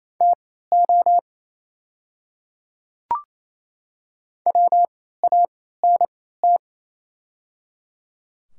Morse code, heard as a keyed tone, about 28 wpm, spaced as for 16 wpm, 690 Hz. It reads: TO WANT
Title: 3x Character Spacing - Top 200 Words - 28wpm